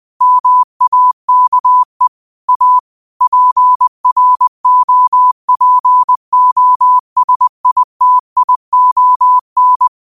Morse code, heard as a keyed tone, about 20 wpm, 1000 Hz. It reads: MAKE A PROPOSITION